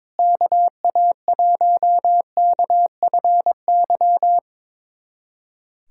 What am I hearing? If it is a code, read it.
Morse code, heard as a keyed tone, about 21 wpm, 700 Hz. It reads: KA1KFY
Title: USA Callsign DE Callsign 22wpm